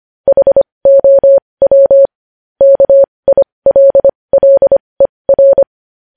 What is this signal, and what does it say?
Morse code, heard as a keyed tone, about 25 wpm, spaced as for 20 wpm, 550 Hz. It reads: HOW KILLER
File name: podcast / News headlines-1x-25x20